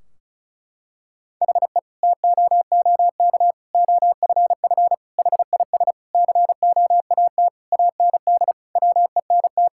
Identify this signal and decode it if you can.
Morse code, heard as a keyed tone, about 36 wpm, 700 Hz. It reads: HE TOOK OFF HIS COAT AND WENT
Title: Sentences from Top 1600 Words 35wpm